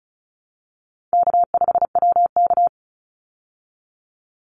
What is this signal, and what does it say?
Morse code, heard as a keyed tone, about 34 wpm, 700 Hz. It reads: K5WK
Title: Easy Call Signs Repeated 3x - 35wpm